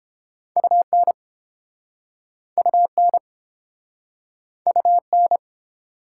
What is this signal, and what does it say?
Morse code, heard as a keyed tone, about 32 wpm, 700 Hz. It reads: UN UN UN